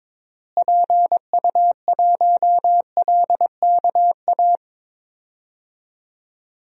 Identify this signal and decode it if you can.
Morse code, heard as a keyed tone, about 22 wpm, 700 Hz. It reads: PU1LKA
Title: DXCC-100 Call Signs - 6 Characters - Repeated 3x - 22wpm